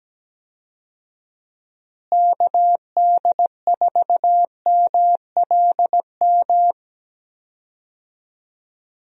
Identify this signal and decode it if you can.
Morse code, heard as a keyed tone, about 17 wpm, 700 Hz. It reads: KD4MLM
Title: USA Call Signs - 6 Characters - Repeated 2x - 17wpm